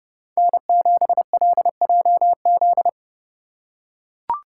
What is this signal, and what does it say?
Morse code, heard as a keyed tone, about 31 wpm, 700 Hz. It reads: N7LJZ
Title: USA Call Signs - 2x Character Spacing - 30wpm